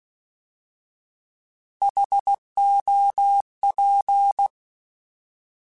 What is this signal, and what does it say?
Morse code, heard as a keyed tone, about 16 wpm, 800 Hz. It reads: HOP
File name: Alphabetical / Alphabetical mp3s / Hs @15